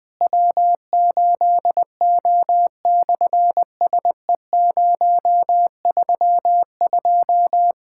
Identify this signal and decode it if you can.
Morse code, heard as a keyed tone, about 20 wpm, 700 Hz. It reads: W8O/SE032
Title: SOTA QSO Practice 20wpm